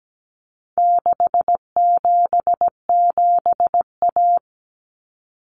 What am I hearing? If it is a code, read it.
Morse code, heard as a keyed tone, about 17 wpm, 700 Hz. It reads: 677A
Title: Common Cut Numbers - 4-Digit 17wpm